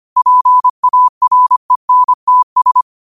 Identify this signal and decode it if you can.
Morse code, heard as a keyed tone, about 25 wpm, 1000 Hz. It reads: PARENTS
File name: qotdfiles / 25wpm